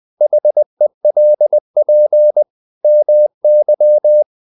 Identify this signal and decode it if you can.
Morse code, heard as a keyed tone, about 20 wpm, 600 Hz. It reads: HELP MY
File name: the old man and the sea / the old man and the sea part 1 20wpm